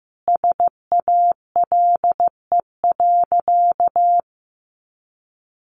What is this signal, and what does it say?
Morse code, heard as a keyed tone, about 15 wpm, 700 Hz. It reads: SALE.
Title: Sentences from Top 1400 Words 15wpm